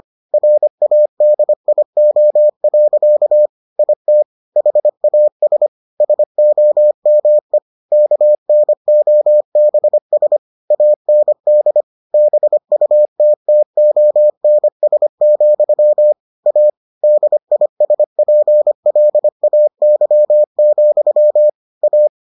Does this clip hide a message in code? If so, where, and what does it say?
Yes: Morse code from 0.3 to 22.1 s, heard as a keyed tone, about 25 wpm, 600 Hz. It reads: RADIO. IT HAS SOME KNOBS AND BUTTONS, A DISPLAY, A